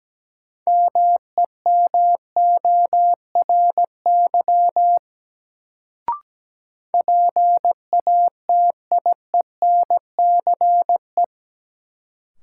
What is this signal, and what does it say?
Morse code, heard as a keyed tone, about 17 wpm, 700 Hz. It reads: MEMORY PATIENCE
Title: Sentences from Top 1300 Words - New Word Drill - 17wpm